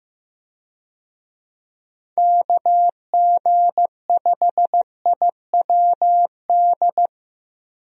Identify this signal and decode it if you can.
Morse code, heard as a keyed tone, about 16 wpm, 700 Hz. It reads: KG5IWD
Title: USA Call Signs - 6 Characters - Repeated 2x - 15wpm